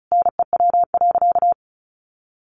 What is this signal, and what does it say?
Morse code, heard as a keyed tone, about 34 wpm, 700 Hz. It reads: NEW.
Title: Sentences from Top 1100 Words 35wpm